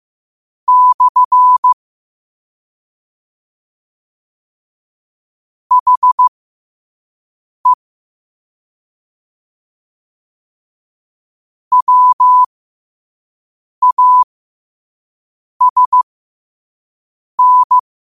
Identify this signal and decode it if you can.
Morse code, heard as a keyed tone, about 15 wpm, spaced as for 5 wpm, 1000 Hz. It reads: / HE WASN
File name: qotdfiles / 5wpm